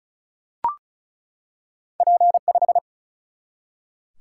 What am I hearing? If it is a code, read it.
Morse code, heard as a keyed tone, about 36 wpm, 700 Hz. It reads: P 5
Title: Single-Letter Single-Number 35wpm